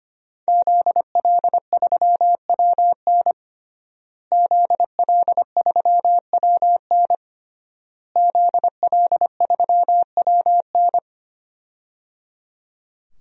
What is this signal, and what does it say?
Morse code, heard as a keyed tone, about 25 wpm, 700 Hz. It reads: ZL3WN ZL3WN ZL3WN
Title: DXCC-100 Call Signs - 5 Characters - Repeated 3x - 25wpm